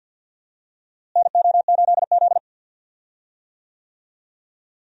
Morse code, heard as a keyed tone, about 52 wpm, 700 Hz. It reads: NO8Z
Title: USA Call Signs - 4 Characters - Repeated 2x - 50wpm